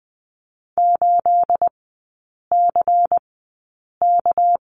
Morse code, heard as a keyed tone, about 20 wpm, spaced as for 8 wpm, 690 Hz. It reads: 8CK
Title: 212 Lesson 34e - Number 8 - Three Characters 20wpm